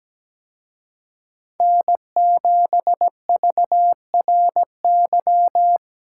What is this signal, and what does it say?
Morse code, heard as a keyed tone, about 20 wpm, 700 Hz. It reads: N7VRY
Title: Speed Racing - USA 1x3 Call Signs 17wpm